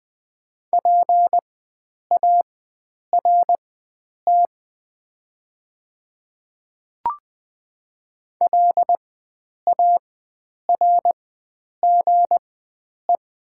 Morse code, heard as a keyed tone, about 20 wpm, spaced as for 9 wpm, 700 Hz. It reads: PART LARGE
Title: 4x Character Spacing - Top 300 Words - 20wpm